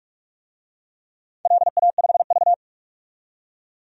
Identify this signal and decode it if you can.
Morse code, heard as a keyed tone, about 46 wpm, 700 Hz. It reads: LA5V
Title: DXCC-100 Call Signs - 4 Characters - 45wpm